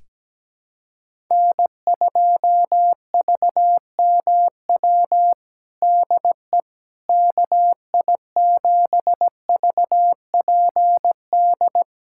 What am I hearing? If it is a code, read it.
Morse code, heard as a keyed tone, about 17 wpm, 700 Hz. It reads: N2VMW DE KI7VPD
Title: USA Callsign DE Callsign 17wpm